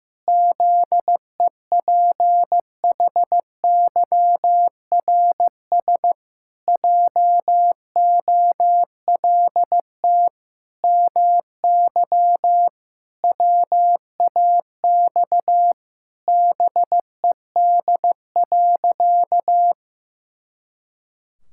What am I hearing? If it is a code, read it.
Morse code, heard as a keyed tone, about 15 wpm, 700 Hz. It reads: ZEPHYRS JOLT MY WAX BED.
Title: Pangram Challenge v2 - 15wpm